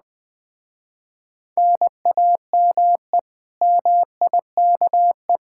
Morse code, heard as a keyed tone, about 20 wpm, 700 Hz. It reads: NAME MIKE
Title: Speed-Racing Common QSO Phrases - 17wpm